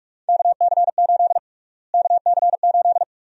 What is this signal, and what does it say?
Morse code, heard as a keyed tone, about 46 wpm, 700 Hz. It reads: KC8 KC8
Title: USA Call Sign Prefixes - Repeated 2x - 45wpm